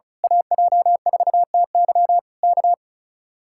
Morse code, heard as a keyed tone, about 33 wpm, 700 Hz. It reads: AJ4TY K
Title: Speed-Racing Common QSO Phrases Encore - 35wpm